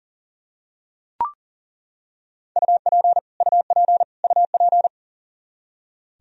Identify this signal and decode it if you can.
Morse code, heard as a keyed tone, about 40 wpm, 700 Hz. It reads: UP UP UP
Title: Top 100 Words Repeated 40wpm